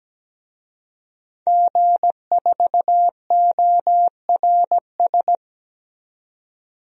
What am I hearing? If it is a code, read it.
Morse code, heard as a keyed tone, about 17 wpm, 700 Hz. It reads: G4ORS
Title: DXCC-100 Call Signs - 5 Characters - 17wpm